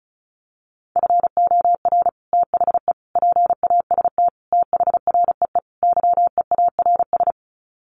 Morse code, heard as a keyed tone, about 34 wpm, 700 Hz. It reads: FOR THE PAST THREE YEARS
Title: Sets of 5 Words Encore2 35wpm